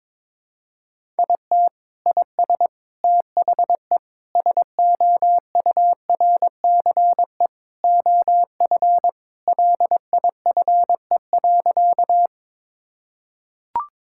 Morse code, heard as a keyed tone, about 22 wpm, 700 Hz. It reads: IT IS THE SOURCE OF LIFE.
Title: Sentences from Top 1000 Words - Review Part 5 - 22wpm